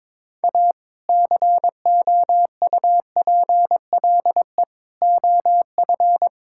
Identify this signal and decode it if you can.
Morse code, heard as a keyed tone, about 22 wpm, 700 Hz. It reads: A COUPLE OF
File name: practice / 22wpm / Build A Phrase - 5-Word Phrases - 22wpm